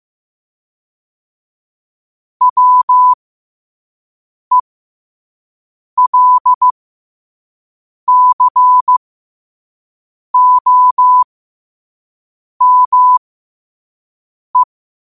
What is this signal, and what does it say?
Morse code, heard as a keyed tone, about 15 wpm, spaced as for 5 wpm, 1000 Hz. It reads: WELCOME
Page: New quotes every day in morse code at 5 Words per minute.